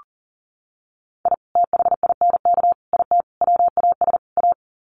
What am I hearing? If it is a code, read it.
Morse code, heard as a keyed tone, about 40 wpm, 700 Hz. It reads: I THINK IT WAS A
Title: Sets of 5 Words 40wpm